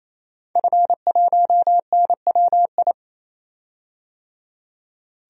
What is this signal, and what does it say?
Morse code, heard as a keyed tone, about 27 wpm, 700 Hz. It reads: F1NWI
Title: DXCC-100 Call Signs - 5 Characters - 28wpm